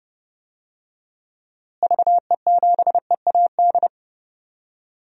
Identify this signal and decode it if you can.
Morse code, heard as a keyed tone, about 30 wpm, 700 Hz. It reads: VE7EAD
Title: DXCC-100 Call Signs - 6 Characters - Repeated 3x - 30wpm